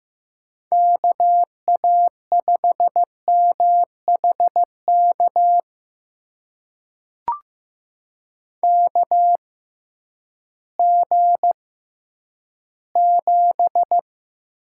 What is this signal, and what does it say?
Morse code, heard as a keyed tone, about 15 wpm, spaced as for 10 wpm, 700 Hz. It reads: KA5MHK KG7
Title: USA Call Signs - 6 Characters - 6x Character Spacing - 15wpm